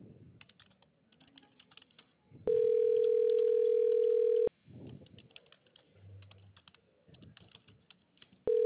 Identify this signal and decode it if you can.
Ringback tone (North America), 440 + 480 Hz.